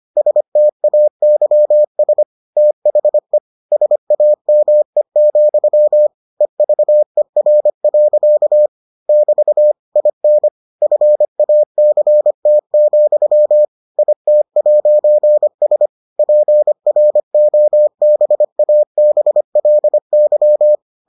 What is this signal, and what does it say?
Morse code, heard as a keyed tone, about 25 wpm, 600 Hz. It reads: STAYS THE SAME, EVER. <BT> IN FACT, IT'S PROBABLY